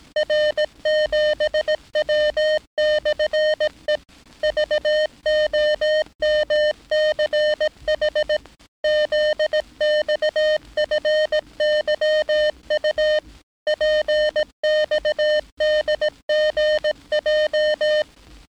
Morse code, heard as a keyed tone, about 18 wpm, 610 Hz. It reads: R7W/E VOMCH ZXFYU PXDGJ
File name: training / NAVPERS 11049to11060 / NAVPERS 11059 RA RB